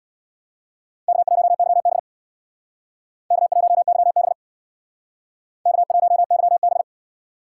Morse code, heard as a keyed tone, about 60 wpm, 700 Hz. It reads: K1YB K1YB K1YB
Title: Speed Racing - USA Call Signs 50wpm